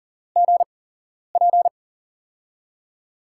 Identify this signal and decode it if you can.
Morse code, heard as a keyed tone, about 38 wpm, spaced as for 11 wpm, 700 Hz. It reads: GP
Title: USA Call Signs - 5 Characters - 8x Character Spacing - 40wpm